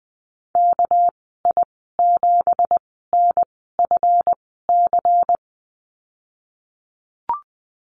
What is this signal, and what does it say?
Morse code, heard as a keyed tone, about 20 wpm, spaced as for 15 wpm, 700 Hz. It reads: KI7NFC
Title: 219 Lesson 35g - Number 0 - Call Signs 20wpm